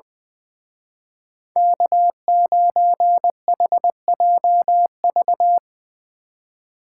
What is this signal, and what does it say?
Morse code, heard as a keyed tone, about 23 wpm, 700 Hz. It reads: K9HJV
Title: Speed Racing - USA 1x3 Call Signs 20wpm